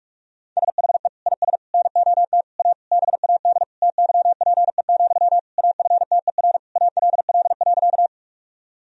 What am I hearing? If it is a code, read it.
Morse code, heard as a keyed tone, about 46 wpm, 700 Hz. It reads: SHE IS NOT A BAD TYPE, AFTER ALL.
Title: Sentences from Top 1000 Words - Review Part 3 - 45wpm